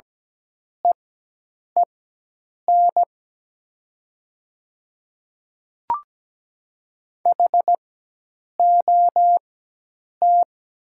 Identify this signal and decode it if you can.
Morse code, heard as a keyed tone, about 17 wpm, spaced as for 8 wpm, 700 Hz. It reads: EEN HOT
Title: 4x Character Spacing - Top 100 Words - 17wpm